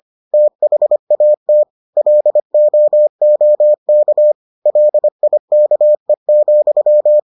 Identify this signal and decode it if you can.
Morse code, heard as a keyed tone, about 25 wpm, 600 Hz. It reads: THAT LOOK LIKE,